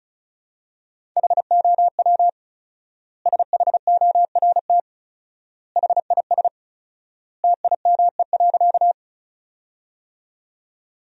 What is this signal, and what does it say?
Morse code, heard as a keyed tone, about 34 wpm, 700 Hz. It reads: HOW SHORT HIS TIME.
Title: 4x Word-Spacing - Sentences from Top 400 Words - 35wpm